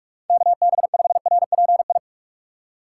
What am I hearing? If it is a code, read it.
Morse code, heard as a keyed tone, about 48 wpm, 700 Hz. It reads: KB5RPI